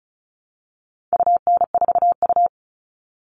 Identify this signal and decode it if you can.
Morse code, heard as a keyed tone, about 34 wpm, 700 Hz. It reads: UN4U
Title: All Cut Numbers - 4-Digit 35wpm